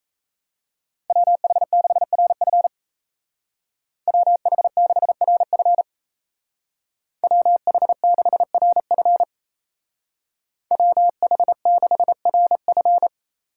Morse code, heard as a keyed tone, about 34 wpm, 700 Hz. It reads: WH6RF WH6RF WH6RF WH6RF
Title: Speed Racing - USA 2x2 Call Signs 28wpm